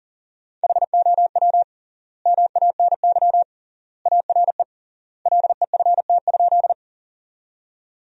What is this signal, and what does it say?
Morse code, heard as a keyed tone, about 37 wpm, 700 Hz. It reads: HOW MANY ARE LEFT?
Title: 3x Word-Spacing - Sentences from Top 300 Words - 40wpm